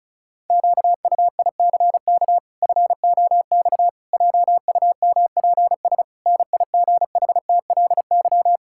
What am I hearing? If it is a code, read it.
Morse code, heard as a keyed tone, about 36 wpm, 700 Hz. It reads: QUICK FOX JUMPS NIGHTLY
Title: Pangram Challenge v2 - 35wpm